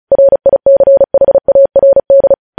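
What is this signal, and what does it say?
Morse code, heard as a keyed tone, about 36 wpm, 550 Hz. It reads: RICHARD
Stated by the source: Morse code news headlines sent at 35 wpm.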